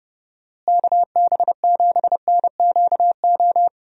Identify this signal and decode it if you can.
Morse code, heard as a keyed tone, about 30 wpm, 700 Hz. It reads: KB7NQO
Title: USA Call Signs - 6 Characters - Repeated 3x - 30wpm